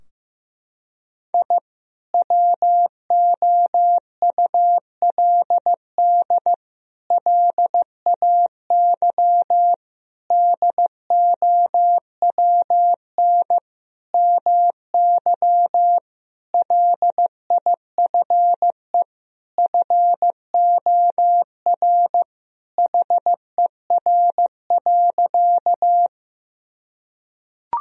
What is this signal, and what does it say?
Morse code, heard as a keyed tone, about 16 wpm, 700 Hz. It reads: I WOULD LAY DOWN MY LIFE FOR HER.
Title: Sentences from Top 1000 Words - Review Part 2 - 15wpm